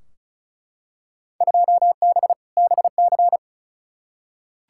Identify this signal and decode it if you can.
Morse code, heard as a keyed tone, about 36 wpm, 700 Hz. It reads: 2B BC
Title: Field Day Exchanges Encore - 35wpm